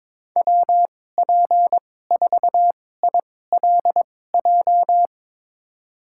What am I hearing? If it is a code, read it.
Morse code, heard as a keyed tone, about 22 wpm, spaced as for 16 wpm, 700 Hz. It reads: WP4ILJ
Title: USA Call Signs - 6 Characters - 2x Character Spacing - 22wpm